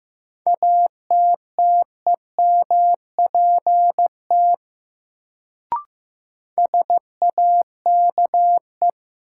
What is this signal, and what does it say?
Morse code, heard as a keyed tone, about 15 wpm, 700 Hz. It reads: ATTEMPT SAKE
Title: Sets of 4 Words - New Word Drill - 15wpm